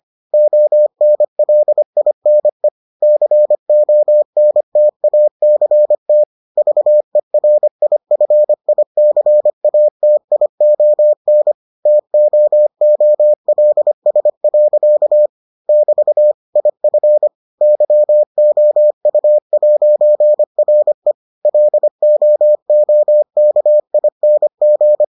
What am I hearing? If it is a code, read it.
Morse code, heard as a keyed tone, about 25 wpm, 600 Hz. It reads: ONLINE CONTACT VERIFICATION TOOLS. <BT> IF YOU'RE LOOKING